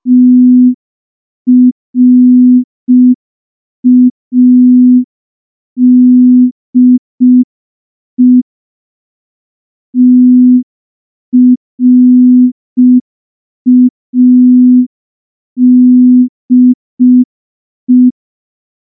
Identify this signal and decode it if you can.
Morse code, heard as a keyed tone, about 5 wpm, 250 Hz. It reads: TRADE TRADE
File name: morse / morse-training-05WPM